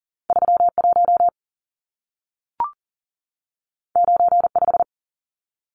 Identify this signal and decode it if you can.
Morse code, heard as a keyed tone, about 40 wpm, 700 Hz. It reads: 31 95
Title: Numbers - 2-Digit 40wpm